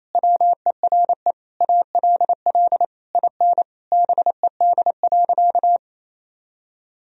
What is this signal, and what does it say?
Morse code, heard as a keyed tone, about 28 wpm, 700 Hz. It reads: WERE ALL IN BED.
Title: Sentences from Top 1000 Words - Review Part 1 - 28wpm